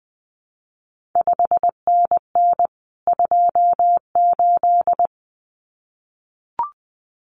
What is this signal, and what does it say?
Morse code, heard as a keyed tone, about 20 wpm, 700 Hz. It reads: 5NN 28
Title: CQ WW DX Contest Exchange Encore - 20wpm